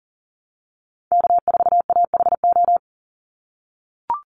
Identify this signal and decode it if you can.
Morse code, heard as a keyed tone, about 40 wpm, 700 Hz. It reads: K4AHO
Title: Call Signs - 40wpm